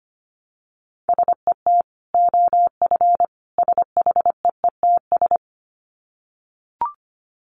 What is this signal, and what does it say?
Morse code, heard as a keyed tone, about 25 wpm, 700 Hz. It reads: SET OF SHEETS
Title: Dit Challenge 25wpm